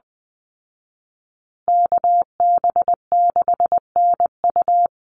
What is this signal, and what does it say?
Morse code, heard as a keyed tone, about 20 wpm, 700 Hz. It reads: KB6NU
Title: Call Signs Repeated 3x - 20wpm